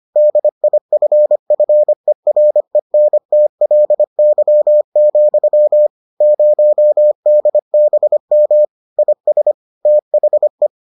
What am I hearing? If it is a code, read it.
Morse code, heard as a keyed tone, about 25 wpm, 600 Hz. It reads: DIFFERENTLY, 0DBM IS THE